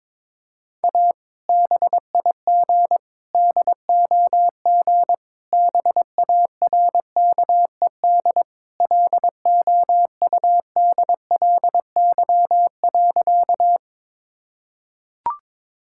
Morse code, heard as a keyed tone, about 23 wpm, 700 Hz. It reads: A BIG DOG BARKED LOUDLY.
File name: practice / 22wpm / Sentences with 2-Word Phrases - 22wpm